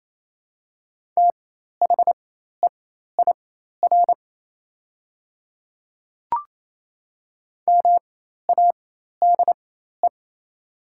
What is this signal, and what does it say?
Morse code, heard as a keyed tone, about 27 wpm, spaced as for 13 wpm, 690 Hz. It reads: THEIR MADE
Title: 4x Character Spacing - Top 200 Words - 28wpm